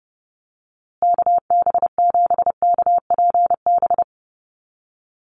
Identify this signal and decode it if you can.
Morse code, heard as a keyed tone, about 30 wpm, 700 Hz. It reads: KB7KPB
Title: USA Call Signs Repeated 3x 30wpm